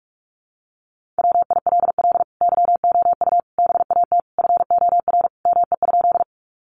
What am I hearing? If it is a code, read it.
Morse code, heard as a keyed tone, about 44 wpm, 700 Hz. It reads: WILL YOU BAT FOR ME?
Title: Sentences from Top 1500 Words 45wpm